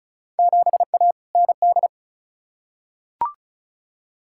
Morse code, heard as a keyed tone, about 36 wpm, 700 Hz. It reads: 7A ND
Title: Field Day Exchanges Encore - 35wpm